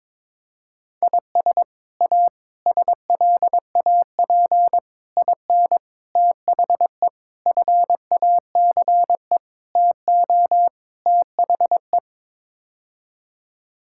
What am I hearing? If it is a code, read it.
Morse code, heard as a keyed tone, about 21 wpm, 700 Hz. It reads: IS A SLAP IN THE FACE TO THE
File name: practice / 22wpm / Sets of 8 Words - 22wpm